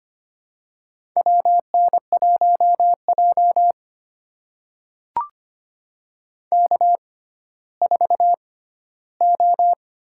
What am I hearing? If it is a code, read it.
Morse code, heard as a keyed tone, about 25 wpm, spaced as for 14 wpm, 700 Hz. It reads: WN1J K4O
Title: USA Call Signs - 4 Characters - 6x Character Spacing - 25wpm